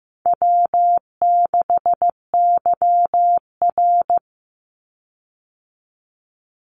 Morse code, heard as a keyed tone, about 15 wpm, 700 Hz. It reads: W6YR
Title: Easy Call Signs Repeated 2x - 15wpm